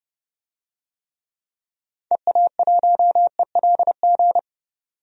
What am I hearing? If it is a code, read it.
Morse code, heard as a keyed tone, about 32 wpm, 700 Hz. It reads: EA1ELG
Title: DXCC-100 Call Signs - 6 Characters - 30wpm